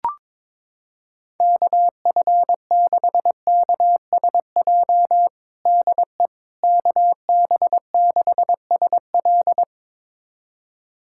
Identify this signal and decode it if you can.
Morse code, heard as a keyed tone, about 22 wpm, 700 Hz. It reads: KF6KSJ DE KB6SL
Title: USA Callsign DE Callsign 22wpm